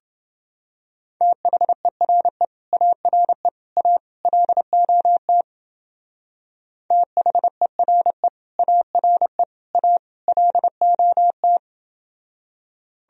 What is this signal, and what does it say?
Morse code, heard as a keyed tone, about 28 wpm, 700 Hz. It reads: THERE ARE A LOT THERE ARE A LOT